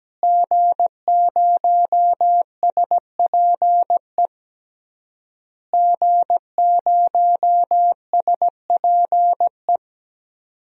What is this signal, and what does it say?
Morse code, heard as a keyed tone, about 17 wpm, 700 Hz. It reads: G0SPE G0SPE
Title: DXCC-100 Call Signs - 5 Characters - Repeated 3x - 17wpm